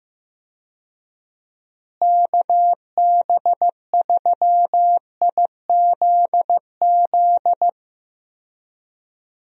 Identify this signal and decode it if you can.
Morse code, heard as a keyed tone, about 15 wpm, 700 Hz. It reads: KB3IZZ
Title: USA Call Signs - 6 Characters - Repeated 2x - 15wpm